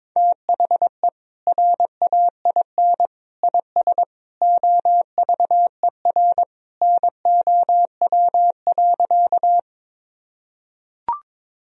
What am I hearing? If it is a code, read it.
Morse code, heard as a keyed tone, about 22 wpm, 700 Hz. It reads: THE RAIN IS OVER NOW.
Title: Sentences from Top 1000 Words - Review Part 1 - 22wpm